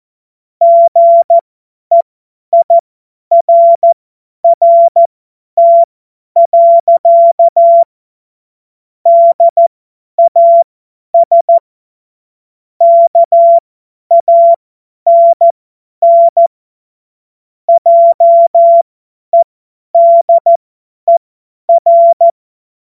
Morse code, heard as a keyed tone, about 14 wpm, spaced as for 10 wpm, 680 Hz. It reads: GEIRRT. DAS KANN JEDER
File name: Morsekurs / Witze / Witz 21.November 2023 50 F70